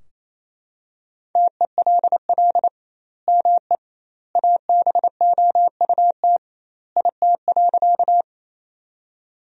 Morse code, heard as a keyed tone, about 27 wpm, 700 Hz. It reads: TELL ME ABOUT IT.
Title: 2x Word-Spacing - Sentences from Top 200 Words - 28wpm